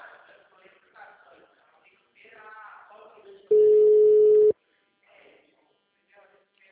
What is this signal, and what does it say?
Ringback tone (Europe, 425 Hz), with speech.